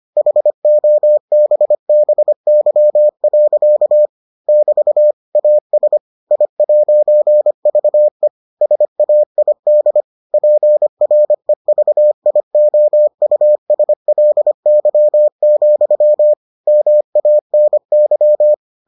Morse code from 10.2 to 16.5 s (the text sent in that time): PREVIOUSLY,